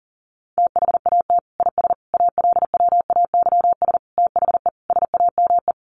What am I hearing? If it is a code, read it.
Morse code, heard as a keyed tone, about 40 wpm, 700 Hz. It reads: THAT IS ALWAYS THE SAME
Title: Sentences from Top 700 Words 40wpm